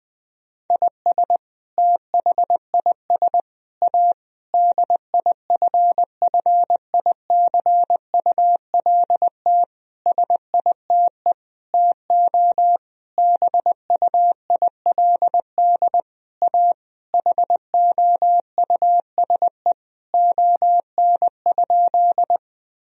Morse code, heard as a keyed tone, about 20 wpm, 700 Hz. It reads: IS THIS A DIFFICULT SITE TO BUILD A HOUSE ON?
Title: Sentences from Top 1000 Words - Review Part 4 - 20wpm